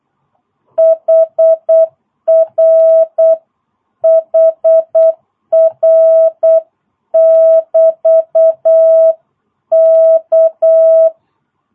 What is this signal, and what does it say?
Morse code, heard as a keyed tone, about 8 wpm, 660 Hz. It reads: HR HR <BT> K
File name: jh8cbh / 2022cw